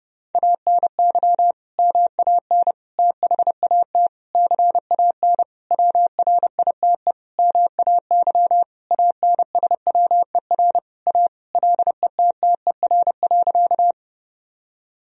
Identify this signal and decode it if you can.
Morse code, heard as a keyed tone, about 30 wpm, 700 Hz. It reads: ANY MAN THAT CAN WRITE MAY ANSWER A LETTER.
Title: Sentences from Top 1000 Words - Review Part 1 - 30wpm